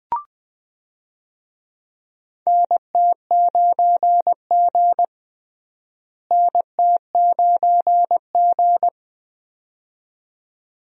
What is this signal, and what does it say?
Morse code, heard as a keyed tone, about 20 wpm, 700 Hz. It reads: NT9G NT9G
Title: USA Call Signs - 4 Characters - Repeated 2x - 20wpm